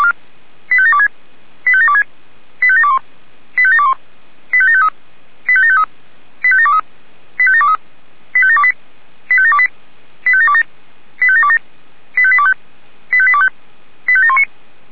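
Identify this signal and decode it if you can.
Five-tone selective call (ZVEI), heard as quick runs of five tones, tones of about 70 ms. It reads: 85726 85726 85721 85721 85753 85753 85723 85723 85728 85728 85727 85727 85825 85825 85719
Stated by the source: ZVEI five-tone selective call 85759, received by SDR.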